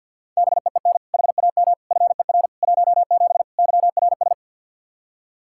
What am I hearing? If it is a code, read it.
Morse code, heard as a keyed tone, about 51 wpm, 700 Hz. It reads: BEEN HAM FER 17 YRS
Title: K1USN SST QSO Practice 50wpm